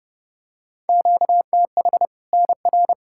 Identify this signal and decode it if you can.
Morse code, heard as a keyed tone, about 30 wpm, 700 Hz. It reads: QTH NR
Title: Practice QSO - Exchange 1 - Beginning - 30wpm